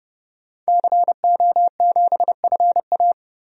Morse code, heard as a keyed tone, about 30 wpm, 700 Hz. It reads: CO7FA
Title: DXCC-100 Call Signs - 5 Characters - 30wpm